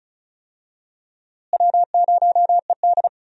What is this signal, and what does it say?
Morse code, heard as a keyed tone, about 36 wpm, 700 Hz. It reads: W0ED
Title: USA Call Signs - 4 Characters - Repeated 3x - 35wpm